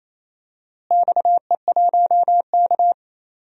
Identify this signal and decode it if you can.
Morse code, heard as a keyed tone, about 28 wpm, 700 Hz. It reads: XE1K
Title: DXCC-100 Call Signs - 4 Characters - Repeated 2x - 28wpm